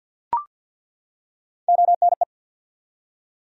Morse code, heard as a keyed tone, about 49 wpm, 700 Hz. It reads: ONE
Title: Top 100 Words 50wpm